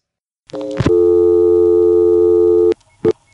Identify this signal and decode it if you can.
Dial tone (North America), 350 + 440 Hz.